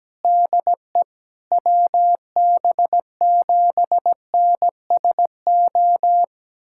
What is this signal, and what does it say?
Morse code, heard as a keyed tone, about 17 wpm, 700 Hz. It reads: DE WB7NSO
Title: USA Callsign DE Callsign 17wpm